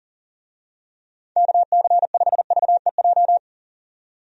Morse code, heard as a keyed tone, about 40 wpm, 700 Hz. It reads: KC5VEJ
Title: USA Call Signs - 6 Characters - 40wpm